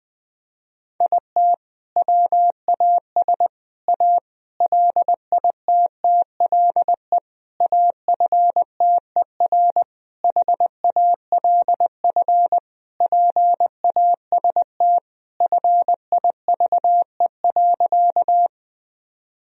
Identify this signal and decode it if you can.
Morse code, heard as a keyed tone, about 20 wpm, 700 Hz. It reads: IT WAS A LITTLE AFTER HALF PAST FIVE.
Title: Sentences from Top 1000 Words - Review Part 3 - 20wpm